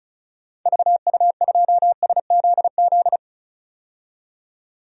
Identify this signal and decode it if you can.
Morse code, heard as a keyed tone, about 34 wpm, 700 Hz. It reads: VU2SZZ
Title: DXCC-100 Call Signs - 6 Characters - 35wpm